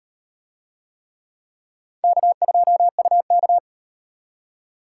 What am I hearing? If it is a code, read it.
Morse code, heard as a keyed tone, about 36 wpm, 700 Hz. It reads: K2UK